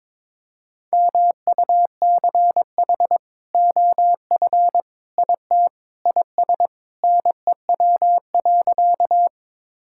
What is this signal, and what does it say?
Morse code, heard as a keyed tone, about 22 wpm, 700 Hz. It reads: MUCH OF IT IS NEW.
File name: practice / 22wpm / Sentences with 2-Word Phrases - 22wpm